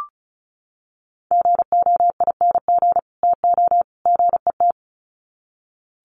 Morse code, heard as a keyed tone, about 34 wpm, 700 Hz. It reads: GOING TO GET
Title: Sets of 3 Words 35wpm